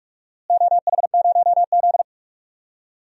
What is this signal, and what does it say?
Morse code, heard as a keyed tone, about 46 wpm, 700 Hz. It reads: OH0Z
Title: DXCC-100 Call Signs - 4 Characters - Repeated 2x - 45wpm